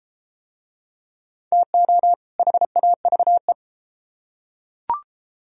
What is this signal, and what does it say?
Morse code, heard as a keyed tone, about 29 wpm, 700 Hz. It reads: TO HAVE